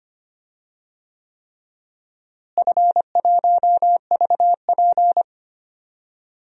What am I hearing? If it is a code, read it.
Morse code, heard as a keyed tone, about 25 wpm, 700 Hz. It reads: F1VP
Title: DXCC-100 Call Signs - 4 Characters - Repeated 3x - 25wpm